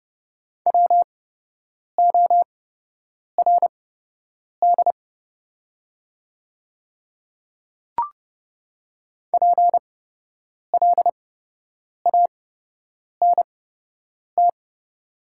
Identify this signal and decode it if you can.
Morse code, heard as a keyed tone, about 30 wpm, spaced as for 8 wpm, 700 Hz. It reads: WORD PLANT
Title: 8x Character Spacing - Top 300 Words - 30wpm